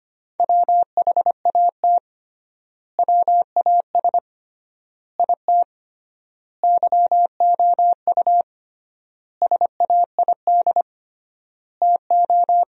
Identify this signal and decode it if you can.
Morse code, heard as a keyed tone, about 25 wpm, 700 Hz. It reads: WHAT WAS IT YOU SAID TO
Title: 3x Word-Spacing - Sentences from Top 200 Words - 25wpm